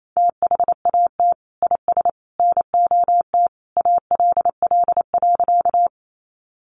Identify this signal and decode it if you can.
Morse code, heard as a keyed tone, about 28 wpm, 700 Hz. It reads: THAT IS NOT ALL.
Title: Sentences from Top 600 Words 28wpm